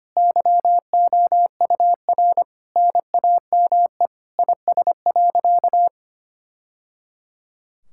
Morse code, heard as a keyed tone, about 25 wpm, 700 Hz. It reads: YOUR NAME IS.
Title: Sentences from Top 1000 Words - Review Part 6 - 25wpm